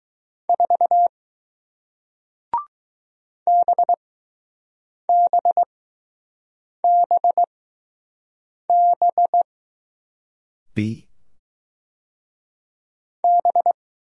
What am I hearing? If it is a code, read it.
Morse code, heard as a keyed tone, about 20 wpm, spaced as for 6 wpm, 700 Hz. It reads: 4 BBBB B